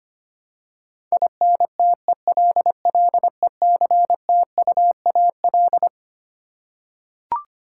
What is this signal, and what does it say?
Morse code, heard as a keyed tone, about 25 wpm, 700 Hz. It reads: INTELLECTUAL
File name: Twelve Letter Words - 25wpm